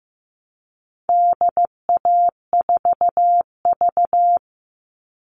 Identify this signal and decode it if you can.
Morse code, heard as a keyed tone, about 15 wpm, 700 Hz. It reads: DA4V
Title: All Cut Numbers - 4-Digit 15wpm